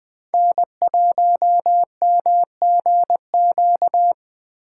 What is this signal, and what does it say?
Morse code, heard as a keyed tone, about 20 wpm, 700 Hz. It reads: N1MGQ
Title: Speed Racing - USA 1x3 Call Signs 17wpm